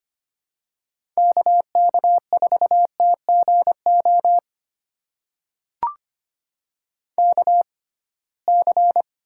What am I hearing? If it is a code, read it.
Morse code, heard as a keyed tone, about 25 wpm, spaced as for 19 wpm, 700 Hz. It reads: KK4TGO KC
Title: USA Call Signs - 6 Characters - 6x Character Spacing - 25wpm